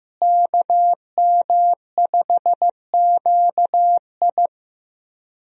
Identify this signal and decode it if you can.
Morse code, heard as a keyed tone, about 15 wpm, 700 Hz. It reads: KM5QI
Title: USA Call Signs - 5 Characters - Repeated 3x - 15wpm